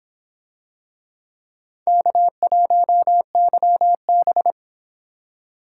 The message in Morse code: K1YB

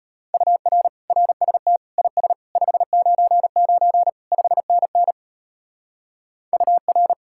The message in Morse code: UR RST IS 599 5NN UR

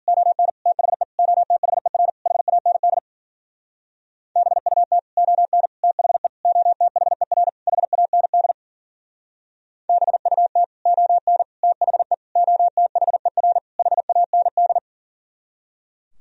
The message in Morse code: ON THE OTHER HAND BUT ON THE OTHER HAND BUT ON THE OTHER HAND

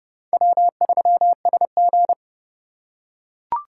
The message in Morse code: W3SG

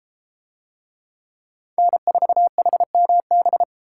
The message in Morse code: N4HMB

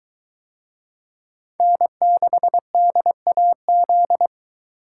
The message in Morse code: N6DAZ

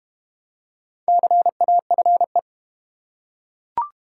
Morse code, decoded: CAFE

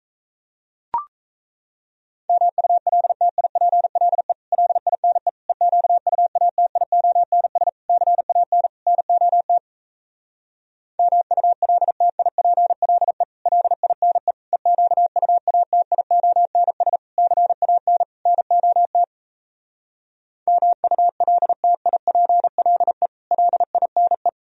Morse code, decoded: MULTIPLE LINE EQUATIONS CAN NOT MULTIPLE LINE EQUATIONS CAN NOT MULTIPLE LINE

